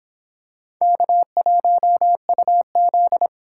Morse code, heard as a keyed tone, about 26 wpm, 700 Hz. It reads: K1UZ